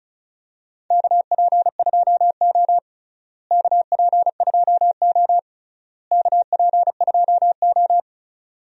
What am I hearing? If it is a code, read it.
Morse code, heard as a keyed tone, about 36 wpm, 700 Hz. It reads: KP2O KP2O KP2O
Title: DXCC-100 Call Signs - 4 Characters - Repeated 3x - 35wpm